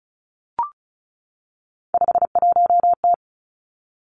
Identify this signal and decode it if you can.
Morse code, heard as a keyed tone, about 36 wpm, 700 Hz. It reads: 51T
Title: Three Challenging Letter-Number Combinations 35wpm